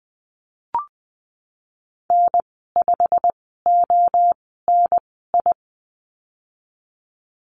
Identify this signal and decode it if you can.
Morse code, heard as a keyed tone, about 20 wpm, spaced as for 14 wpm, 700 Hz. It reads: N5ONI
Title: 161 Lesson 26g - Letter g - Call Signs 20wpm